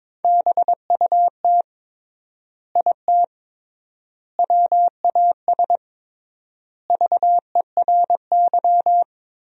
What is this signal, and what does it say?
Morse code, heard as a keyed tone, about 22 wpm, 700 Hz. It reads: BUT IT WAS VERY
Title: 3x Word-Spacing - Sentences from Top 300 Words - 22wpm